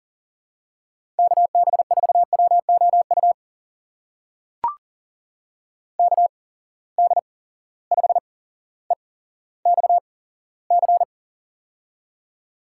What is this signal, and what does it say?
Morse code, heard as a keyed tone, about 38 wpm, spaced as for 17 wpm, 700 Hz. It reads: KB4WOU KD5EXC
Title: USA Call Signs - 6 Characters - 8x Character Spacing - 40wpm